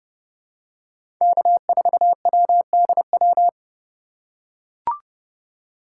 Morse code, heard as a keyed tone, about 30 wpm, 700 Hz. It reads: K4WDW
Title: USA Call Signs - 5 Characters - 4x Character Spacing - 30wpm